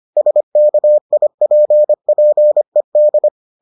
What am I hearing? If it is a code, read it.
Morse code, heard as a keyed tone, about 25 wpm, 600 Hz. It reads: SKIPPED